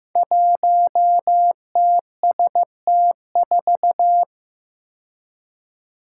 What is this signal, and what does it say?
Morse code, heard as a keyed tone, about 15 wpm, 700 Hz. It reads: 1TST4
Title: Five Challenging Letter-Number Combinations 15wpm